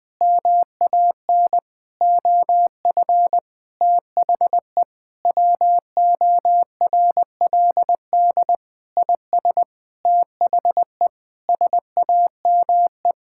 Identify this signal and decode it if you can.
Morse code, heard as a keyed tone, about 20 wpm, 700 Hz. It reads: MAN OF THE WORLD IS THE SAME
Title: Sentences from Top 1600 Words 20wpm